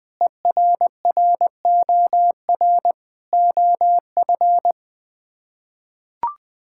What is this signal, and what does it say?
Morse code, heard as a keyed tone, about 20 wpm, 700 Hz. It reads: ERROR OF
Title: Sets of 6 Words - 20wpm